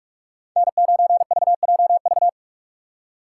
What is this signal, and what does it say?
Morse code, heard as a keyed tone, about 44 wpm, 700 Hz. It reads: N9VJV
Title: USA Call Signs - 3x Character Spacing - 45wpm